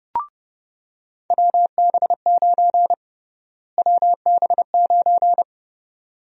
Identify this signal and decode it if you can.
Morse code, heard as a keyed tone, about 30 wpm, 700 Hz. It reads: WB9 WB9
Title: USA Call Sign Prefixes - Repeated 2x - 30wpm